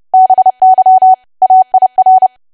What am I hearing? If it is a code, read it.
Morse code, heard as a keyed tone, about 30 wpm, 750 Hz. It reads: BY AIR
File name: cwa / W1AW-Level3 / 30WPM